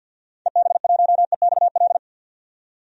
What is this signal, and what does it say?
Morse code, heard as a keyed tone, about 47 wpm, 700 Hz. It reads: EB1EXL
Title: DXCC-100 Call Signs - 6 Characters - 50wpm